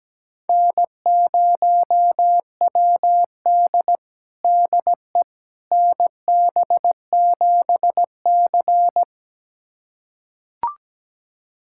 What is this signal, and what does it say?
Morse code, heard as a keyed tone, about 17 wpm, 700 Hz. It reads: N0WD DE NB7C
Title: USA Callsign DE Callsign - 4 Characters - 17wpm